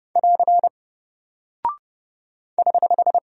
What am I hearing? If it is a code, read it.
Morse code, heard as a keyed tone, about 30 wpm, 700 Hz. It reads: <AR> <HH>
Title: Single Character (Letter:Number:Punctuation:Prosign) 30wpm